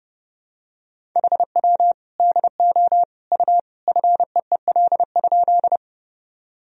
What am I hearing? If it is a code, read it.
Morse code, heard as a keyed tone, about 32 wpm, 700 Hz. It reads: HW DO U FEEL?
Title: Sentences from Top 400 Words and Abbreviations 30wpm